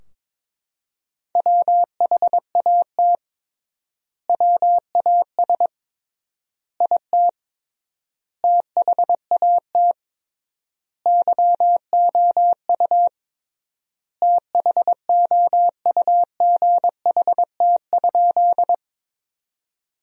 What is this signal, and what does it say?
Morse code, heard as a keyed tone, about 22 wpm, 700 Hz. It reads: WHAT WAS IT THAT YOU THOUGHT?
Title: 3x Word-Spacing - Sentences from Top 300 Words - 22wpm